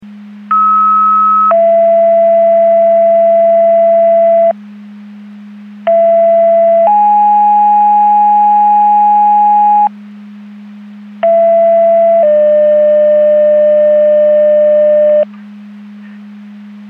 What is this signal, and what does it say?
Two-tone page, heard as a one-second tone, then a three-second tone, about 1290 then 670 Hz, about 670 then 830 Hz, about 670 then 600 Hz. It is heard from 0.5 to 15.2 s.
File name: Portage 154.385 Paging